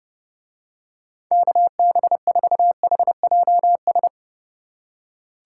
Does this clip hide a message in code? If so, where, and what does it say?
Yes: Morse code from 1.3 to 4.1 s, heard as a keyed tone, about 32 wpm, 700 Hz. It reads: KB4HJS